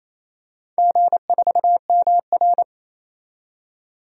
Morse code, heard as a keyed tone, about 27 wpm, 700 Hz. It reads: G4MR
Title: DXCC-100 Call Signs - 4 Characters - 28wpm